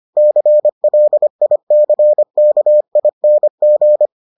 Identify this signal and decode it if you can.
Morse code, heard as a keyed tone, about 25 wpm, 600 Hz. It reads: CLICKING